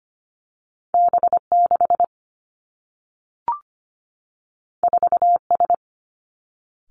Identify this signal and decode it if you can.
Morse code, heard as a keyed tone, about 25 wpm, 700 Hz. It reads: B 6 4 S
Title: Two Challenging Letter-Number Combinations 25wpm